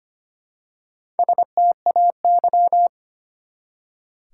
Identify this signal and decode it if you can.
Morse code, heard as a keyed tone, about 25 wpm, 700 Hz. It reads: STAY
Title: Four Letter Words - 25wpm